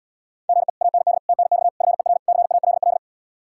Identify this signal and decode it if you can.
Morse code, heard as a keyed tone, about 77 wpm, 700 Hz. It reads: GE AND TNX FER CALL